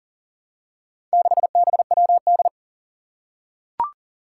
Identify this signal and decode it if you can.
Morse code, heard as a keyed tone, about 40 wpm, 700 Hz. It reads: 6BWD
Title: Four Challenging Letter-Number Combinations 40wpm